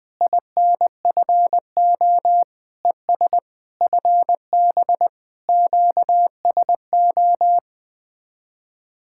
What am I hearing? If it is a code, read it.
Morse code, heard as a keyed tone, about 20 wpm, 700 Hz. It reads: INFO ES FB QSO
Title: Speed-Racing Common QSO Phrases - 17wpm